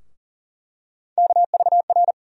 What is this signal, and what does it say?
Morse code, heard as a keyed tone, about 40 wpm, 700 Hz. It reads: KVR
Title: USA Call Sign Postfixes - 3 Letters 40wpm